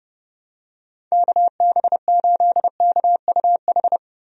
Morse code, heard as a keyed tone, about 30 wpm, 700 Hz. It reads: KB8KUH
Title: USA Call Signs - 6 Characters - Repeated 2x - 30wpm